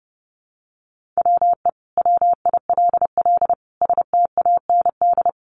Morse code, heard as a keyed tone, about 30 wpm, 700 Hz. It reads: WE WILL STAND
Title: Sentences from Top 1400 Words 30wpm